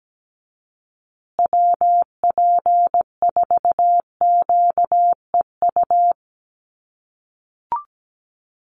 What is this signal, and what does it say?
Morse code, heard as a keyed tone, about 17 wpm, 700 Hz. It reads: WP4QEU
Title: USA Call Signs Repeated 3x 17wpm